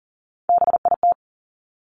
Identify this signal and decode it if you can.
Morse code, heard as a keyed tone, about 40 wpm, 700 Hz. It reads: BIT
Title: Top 751-1000 Words 40wpm